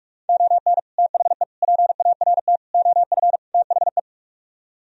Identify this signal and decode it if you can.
Morse code, heard as a keyed tone, about 44 wpm, 700 Hz. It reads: ON THE PART OF THE